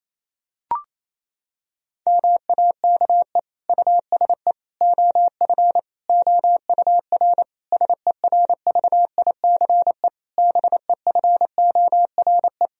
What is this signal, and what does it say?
Morse code, heard as a keyed tone, about 28 wpm, 700 Hz. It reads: MAKE USE OF OUR SERVICE BEFORE
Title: Sentences from Top 1000 Words - Review Part 4 - 28wpm